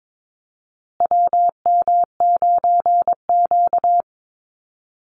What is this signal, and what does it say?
Morse code, heard as a keyed tone, about 22 wpm, 700 Hz. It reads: WM9Q
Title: Call Signs Repeated 3x - 22wpm